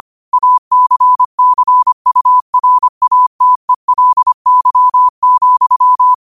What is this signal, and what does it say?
Morse code, heard as a keyed tone, about 25 wpm, 1000 Hz. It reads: ACCURATELY,
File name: qotdfiles / 25wpm